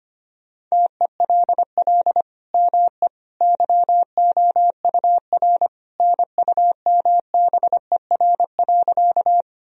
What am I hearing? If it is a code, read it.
Morse code, heard as a keyed tone, about 25 wpm, 700 Hz. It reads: TELL ME YOUR NUMBER.
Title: Sentences from Top 1000 Words - Review Part 1 - 25wpm